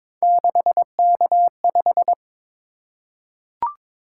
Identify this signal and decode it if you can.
Morse code, heard as a keyed tone, about 22 wpm, 700 Hz. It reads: 6K5
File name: practice / 22wpm / Three Letter-Number Combinations - 22wpm